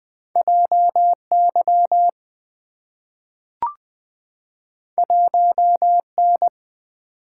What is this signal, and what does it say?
Morse code, heard as a keyed tone, about 20 wpm, 690 Hz. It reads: JY 1N
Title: Two Letter-Number Combinations - 20wpm